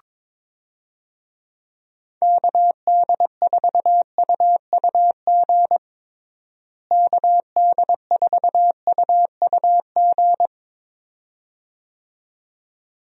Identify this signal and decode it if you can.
Morse code, heard as a keyed tone, about 21 wpm, 700 Hz. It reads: KD4UUG KD4UUG
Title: USA Call Signs - 6 Characters - Repeated 2x - 22wpm